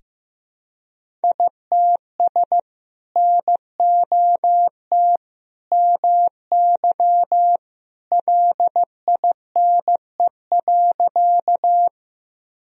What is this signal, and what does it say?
Morse code, heard as a keyed tone, about 16 wpm, 700 Hz. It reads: ITS NOT MY LINE.
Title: Sentences from Top 1000 Words - Review Part 1 - 15wpm